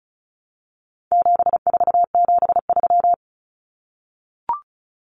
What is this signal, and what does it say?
Morse code, heard as a keyed tone, about 34 wpm, 700 Hz. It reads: 7473
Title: Common Cut Numbers - 4-Digit 35wpm